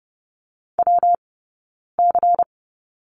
Morse code, heard as a keyed tone, about 30 wpm, 700 Hz. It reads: W C
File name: Lesson 19d - Letter w - Two Characters 30wpm